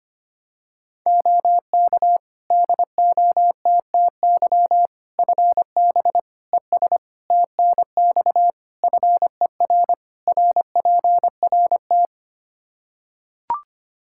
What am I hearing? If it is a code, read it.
Morse code, heard as a keyed tone, about 25 wpm, 700 Hz. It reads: OK DOTTY FB ES TNX FER RPRT
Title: K1USN SST QSO Practice 25wpm